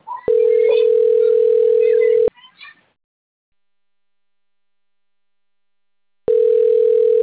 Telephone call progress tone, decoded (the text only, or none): ringback tone